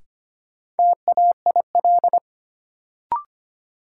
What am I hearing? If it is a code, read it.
Morse code, heard as a keyed tone, about 25 wpm, 700 Hz. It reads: TAIL
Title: Sentences from Top 1000 Words - Review Part 3 - New Word Drill - 25wpm